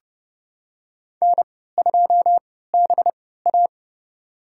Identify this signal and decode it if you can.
Morse code, heard as a keyed tone, about 30 wpm, spaced as for 17 wpm, 700 Hz. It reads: N2BA
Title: USA Call Signs - 4 Characters - 3x Character Spacing - 30wpm